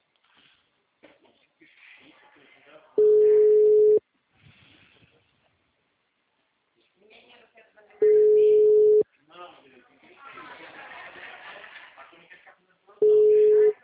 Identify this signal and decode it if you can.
Ringback tone (Europe, 425 Hz), with speech.